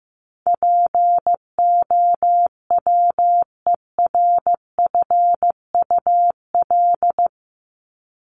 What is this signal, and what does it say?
Morse code, heard as a keyed tone, about 15 wpm, 700 Hz. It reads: POWERFUL
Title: Top 1251-1500 Words 15wpm